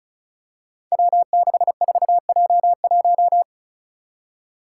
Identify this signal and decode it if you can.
Morse code, heard as a keyed tone, about 36 wpm, 700 Hz. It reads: W64J1
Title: Five Challenging Letter-Number Combinations 35wpm